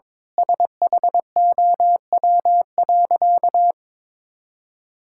Morse code, heard as a keyed tone, about 22 wpm, 700 Hz. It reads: SHOW.
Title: Sentences from Top 1000 Words - Review Part 6 - 22wpm